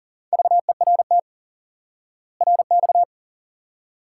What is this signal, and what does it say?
Morse code, heard as a keyed tone, about 37 wpm, 700 Hz. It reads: VERT RX
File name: IQE Course 3 - Lesson 89 - YAGI - Two Elements - 40wpm